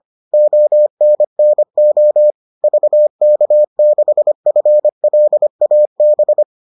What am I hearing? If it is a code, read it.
Morse code, heard as a keyed tone, about 25 wpm, 600 Hz. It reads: ONNO VK6FLAB